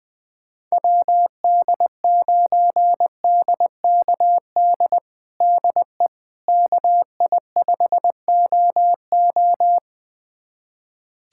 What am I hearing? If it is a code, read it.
Morse code, heard as a keyed tone, about 21 wpm, 700 Hz. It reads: WD9DKD DE KI5OO
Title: USA Callsign DE Callsign 20wpm